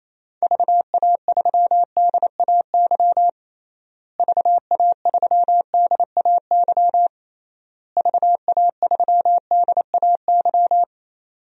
Morse code, heard as a keyed tone, about 28 wpm, 700 Hz. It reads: VA3DAY VA3DAY VA3DAY
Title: DXCC-100 Call Signs - 6 Characters - Repeated 3x - 28wpm